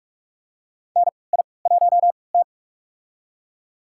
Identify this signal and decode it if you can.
Morse code, heard as a keyed tone, about 46 wpm, spaced as for 26 wpm, 700 Hz. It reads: NI1T
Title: USA Call Signs - 3x Character Spacing - 45wpm